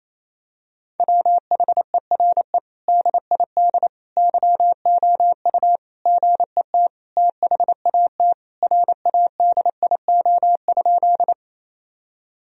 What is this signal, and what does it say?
Morse code, heard as a keyed tone, about 28 wpm, 700 Hz. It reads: WHERE DID YOU GET THAT RADIO?
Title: Sentences from Top 1000 Words - Review Part 3 - 28wpm